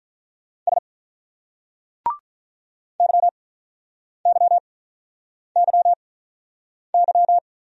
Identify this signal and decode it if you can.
Morse code, heard as a keyed tone, about 44 wpm, spaced as for 8 wpm, 700 Hz. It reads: S YYYY
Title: Speed-Racing Single Letter-Number 35wpm